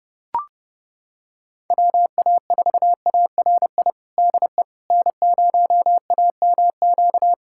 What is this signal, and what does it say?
Morse code, heard as a keyed tone, about 30 wpm, 700 Hz. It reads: WA4ARI DE N0AMQ
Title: USA Callsign DE Callsign 30wpm